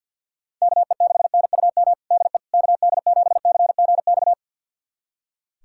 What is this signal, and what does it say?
Morse code, heard as a keyed tone, about 46 wpm, 700 Hz. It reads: KE6NUM DE KD7CGX
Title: USA Callsign DE Callsign - 6 Characters - 50wpm